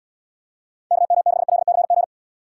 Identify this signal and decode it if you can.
Morse code, heard as a keyed tone, about 79 wpm, 700 Hz. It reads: KM6LCG